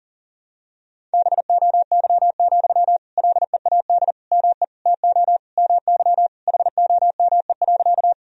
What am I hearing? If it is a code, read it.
Morse code, heard as a keyed tone, about 40 wpm, 700 Hz. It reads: BOY, LEAD ME TO MY HOME.
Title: Sentences from Top 1000 Words - Review Part 2 - 40wpm